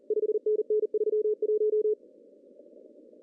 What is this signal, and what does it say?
Morse code, heard as a keyed tone, about 38 wpm, 430 Hz. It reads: 5NN 31
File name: CW / cqww-cw-k5tr